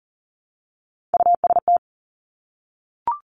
Morse code, heard as a keyed tone, about 42 wpm, 700 Hz. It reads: UST
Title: Three Challenging Letter-Number Combinations 40wpm